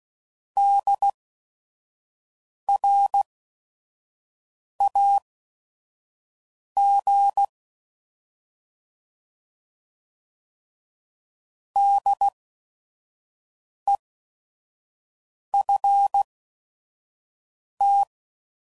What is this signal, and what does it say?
Morse code, heard as a keyed tone, about 16 wpm, spaced as for 5 wpm, 800 Hz. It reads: DRAG DEFT
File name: Prefixes / By First Letter / Ds